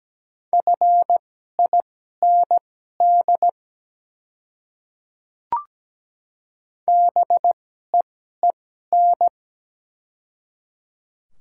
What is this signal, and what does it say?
Morse code, heard as a keyed tone, about 17 wpm, spaced as for 12 wpm, 700 Hz. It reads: FIND BEEN
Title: 2x Character Spacing - Top 200 Words - 17wpm